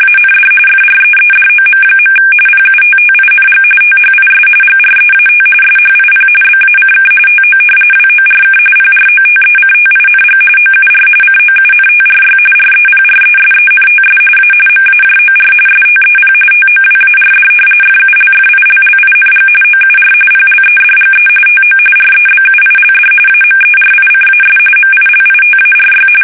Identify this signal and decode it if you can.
RTTY, about 45.45 baud. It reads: AND FREEDOM. PUTTING THIS GREAT SHIP BACK TO WORK PROTECTING OUR COUNTRY REPRESENTS A MAJOR STEP TOWARD FULFILLING OUR PLEDGE TO REBUILD AMERICAS MILITARY
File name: fsk850 / battleshipnj